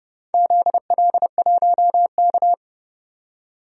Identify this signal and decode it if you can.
Morse code, heard as a keyed tone, about 30 wpm, 700 Hz. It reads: ZL1K
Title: DXCC-100 Call Signs - 4 Characters - 30wpm